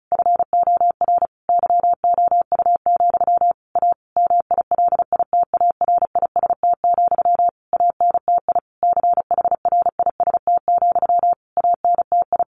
Morse code, heard as a keyed tone, about 34 wpm, 700 Hz. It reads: FOR YOU, A MILITARIST, ANTI CHRIST, ANTI
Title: A Sound Of Thunder 35wpm